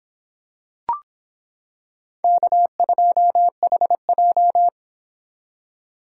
Morse code, heard as a keyed tone, about 26 wpm, 700 Hz. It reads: K2HJ